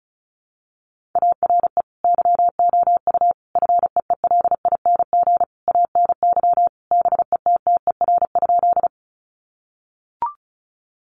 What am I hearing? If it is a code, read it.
Morse code, heard as a keyed tone, about 34 wpm, 700 Hz. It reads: ARE YOU FEELING ANY BETTER?
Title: Sentences from Top 1200 Words 35wpm